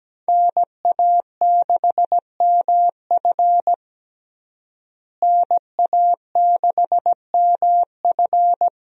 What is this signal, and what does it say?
Morse code, heard as a keyed tone, about 17 wpm, 700 Hz. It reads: NA6MF NA6MF
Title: USA Call Signs - 5 Characters - Repeated 2x - 17wpm